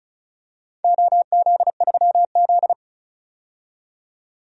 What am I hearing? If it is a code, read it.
Morse code, heard as a keyed tone, about 36 wpm, 690 Hz. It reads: OZ3Z
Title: DXCC-100 Call Signs - 4 Characters - 35wpm